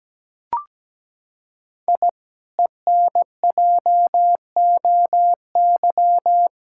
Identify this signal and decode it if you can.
Morse code, heard as a keyed tone, about 17 wpm, 700 Hz. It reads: I ENJOY
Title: Sentences from Top 1000 Words - Review Part 5 - 17wpm